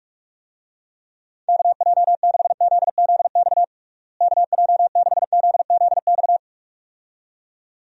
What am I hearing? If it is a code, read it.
Morse code, heard as a keyed tone, about 42 wpm, 700 Hz. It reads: KJ6ZZX KJ6ZZX
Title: USA Call Signs - 6 Characters - Repeated 2x - 45wpm